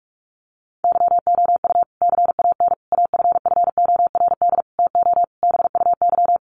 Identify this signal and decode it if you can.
Morse code, heard as a keyed tone, about 44 wpm, 700 Hz. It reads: YOU CAN AFFORD TO BUY
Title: Sentences from Top 1100 Words 45wpm